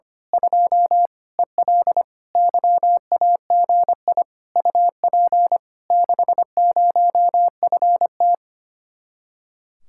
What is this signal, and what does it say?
Morse code, heard as a keyed tone, about 25 wpm, 700 Hz. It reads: 2 EL YAGI UP 60FT
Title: Speed-Racing Common QSO Phrases - 25wpm